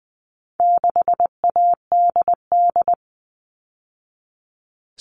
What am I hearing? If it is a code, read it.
Morse code, heard as a keyed tone, about 20 wpm, 700 Hz. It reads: 6ADD
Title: All Cut Numbers - 4-Digit 20wpm